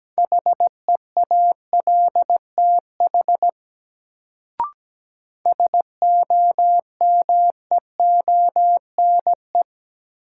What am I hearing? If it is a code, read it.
Morse code, heard as a keyed tone, about 17 wpm, 700 Hz. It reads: HEALTH SOMEONE
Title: Sentences from Top 600 Words - New Word Drill - 17wpm